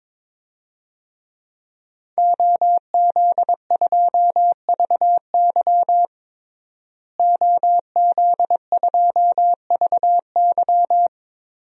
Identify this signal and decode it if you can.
Morse code, heard as a keyed tone, about 21 wpm, 700 Hz. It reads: OZ2VY OZ2VY
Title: DXCC-100 Call Signs - 5 Characters - Repeated 3x - 22wpm